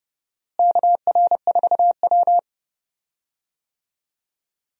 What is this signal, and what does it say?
Morse code, heard as a keyed tone, about 32 wpm, 700 Hz. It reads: KR4W
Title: USA Call Signs - 4 Characters - Repeated 3x - 30wpm